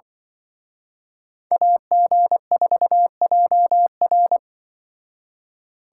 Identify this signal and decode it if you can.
Morse code, heard as a keyed tone, about 24 wpm, 700 Hz. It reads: AG4JR
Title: Speed Racing - USA 2x2 Call Signs 20wpm